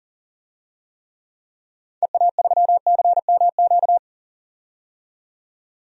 Morse code, heard as a keyed tone, about 40 wpm, 700 Hz. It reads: EA3CMQ
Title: DXCC-100 Call Signs - 6 Characters - 40wpm